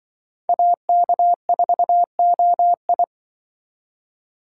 Morse code, heard as a keyed tone, about 24 wpm, 700 Hz. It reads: AK4OI
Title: Speed Racing - USA Call Signs 20wpm